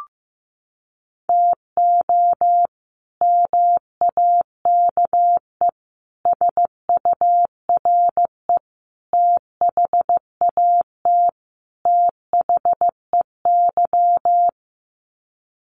Morse code, heard as a keyed tone, about 15 wpm, 700 Hz. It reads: TO MAKE SURE THAT THEY
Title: Sets of 5 Words Encore2 15wpm